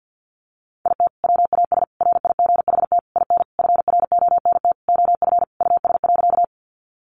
Morse code, heard as a keyed tone, about 50 wpm, 700 Hz. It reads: IT WAS RIGHT IN FRONT OF US.
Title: Sentences from Top 500 Words 50wpm